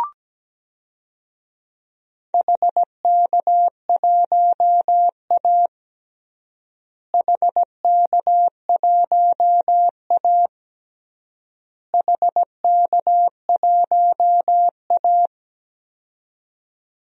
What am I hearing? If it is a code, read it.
Morse code, heard as a keyed tone, about 17 wpm, 700 Hz. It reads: HK1A HK1A HK1A
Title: DXCC-100 Call Signs - 4 Characters - Repeated 3x - 17wpm